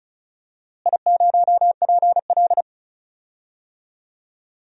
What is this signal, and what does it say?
Morse code, heard as a keyed tone, about 36 wpm, 700 Hz. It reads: I0PL
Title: DXCC-100 Call Signs - 4 Characters - Repeated 2x - 35wpm